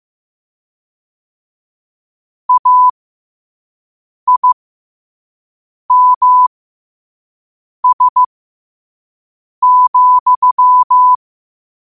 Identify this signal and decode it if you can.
Morse code, heard as a keyed tone, about 15 wpm, spaced as for 5 wpm, 1000 Hz. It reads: AIMS,
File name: qotdfiles / 5wpm